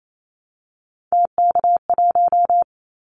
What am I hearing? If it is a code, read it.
Morse code, heard as a keyed tone, about 28 wpm, 700 Hz. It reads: TK1
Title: Callsign Prefixes for Top 100 DXCC - 28wpm